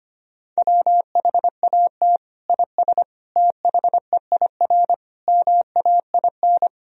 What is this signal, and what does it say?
Morse code, heard as a keyed tone, about 25 wpm, 700 Hz. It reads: WHAT IS THEIR MAIN
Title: Sentences from Top 1000 Words - Review Part 2 - 25wpm